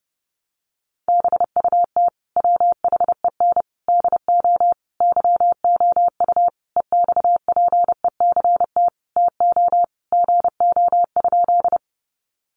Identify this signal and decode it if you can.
Morse code, heard as a keyed tone, about 30 wpm, 700 Hz. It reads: BUT WHEN DO YOU EXPECT TO GO?
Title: Sentences from Top 1100 Words 30wpm